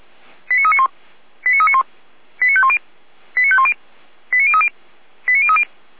Five-tone selective call (ZVEI), heard as quick runs of five tones, tones of about 80 ms. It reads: 89391 89391 89510 89510 89030 89030